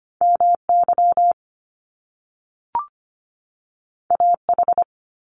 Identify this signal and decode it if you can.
Morse code, heard as a keyed tone, about 25 wpm, 700 Hz. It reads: MY AH
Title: Top Two Letter Combinations in Words 25wpm